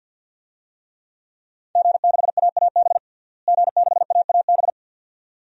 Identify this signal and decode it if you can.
Morse code, heard as a keyed tone, about 52 wpm, 700 Hz. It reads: G6AAB G6AAB
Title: DXCC-100 Call Signs - 5 Characters - Repeated 2x - 50wpm